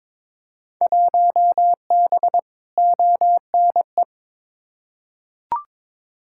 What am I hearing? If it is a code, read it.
Morse code, heard as a keyed tone, about 22 wpm, 700 Hz. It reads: 1B ONE
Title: Field Day Exchanges - 22wpm